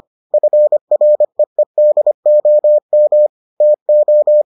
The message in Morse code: FREEDOM TO

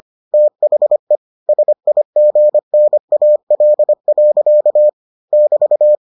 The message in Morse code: THE SIGNAL. <BT>

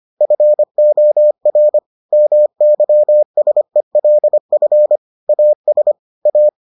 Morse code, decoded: FOR MYSELF AS A

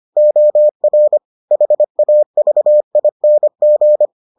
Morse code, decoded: OR HAVING

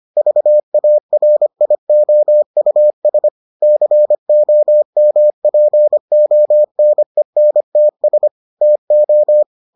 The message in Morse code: VARIOUS COMPONENTS TO

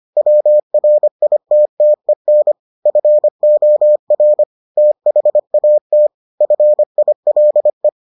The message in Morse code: WRITTEN FOR THAT FILE

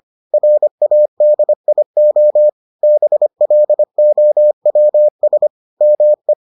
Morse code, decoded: RADIO BLOWS ME

